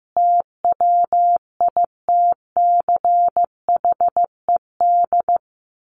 Morse code, heard as a keyed tone, about 15 wpm, 700 Hz. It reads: TWITCHED
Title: A Sound Of Thunder 15wpm